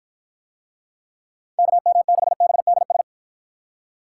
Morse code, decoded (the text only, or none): KM6BDS